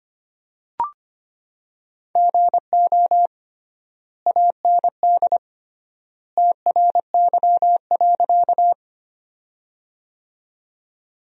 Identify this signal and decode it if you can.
Morse code, heard as a keyed tone, about 25 wpm, 700 Hz. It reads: GO AND TRY.
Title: 3x Word-Spacing - Sentences from Top 200 Words - 25wpm